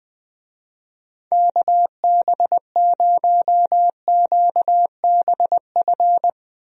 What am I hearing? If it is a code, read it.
Morse code, heard as a keyed tone, about 20 wpm, 700 Hz. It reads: KB0QBF
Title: USA Call Signs - 6 Characters - Repeated 2x - 20wpm